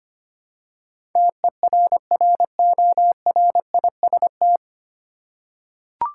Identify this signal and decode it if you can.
Morse code, heard as a keyed tone, about 25 wpm, 700 Hz. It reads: TERRORIST
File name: Nine Letter Words - 25wpm